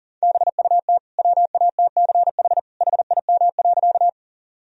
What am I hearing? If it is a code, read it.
Morse code, heard as a keyed tone, about 40 wpm, 700 Hz. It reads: BUT WATCH HIM.
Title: Sentences from Top 1000 Words - Review Part 1 - 40wpm